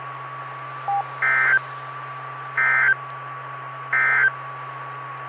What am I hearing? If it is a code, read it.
Emergency Alert System (SAME) tones, heard as harsh buzzing bursts. It is heard from 1.1 to 4.4 s.
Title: Saline, AK SKYWARN Repeater System - W5RHS